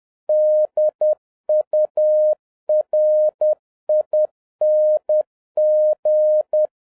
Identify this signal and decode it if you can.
Morse code, heard as a keyed tone, about 10 wpm, 610 Hz. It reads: DURING
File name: 2011-07-mp3 / n46 JOHN K3TN 10wpm